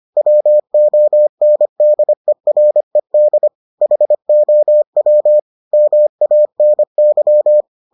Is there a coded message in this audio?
Yes: Morse code from 0.2 to 7.6 s, heard as a keyed tone, about 25 wpm, 600 Hz. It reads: WONDERED HOW MANY